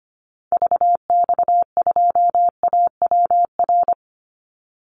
Morse code, heard as a keyed tone, about 25 wpm, 700 Hz. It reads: VX2AWR
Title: Call Signs Repeated 2x - 25wpm